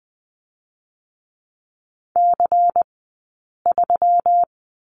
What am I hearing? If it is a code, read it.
Morse code, heard as a keyed tone, about 20 wpm, 700 Hz. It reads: C 3
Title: 137 Lesson 23d - Letter f - Two Characters 20wpm